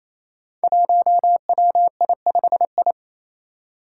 Morse code, heard as a keyed tone, about 28 wpm, 700 Hz. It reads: 1WI5I
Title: Five Challenging Letter-Number Combinations 28wpm